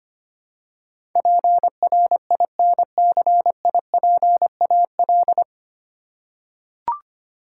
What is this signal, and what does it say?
Morse code, heard as a keyed tone, about 25 wpm, 700 Hz. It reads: PRINCIPAL
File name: practice / 25wpm / 9 Letter Words - First 3 Letters Given - 25wpm